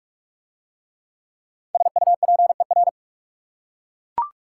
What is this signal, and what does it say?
Morse code, heard as a keyed tone, about 42 wpm, 700 Hz. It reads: SUPER